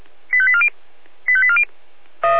Five-tone selective call (ZVEI), heard as quick runs of five tones, tones of about 70 ms. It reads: 85840 85840
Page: SDR/ZVEI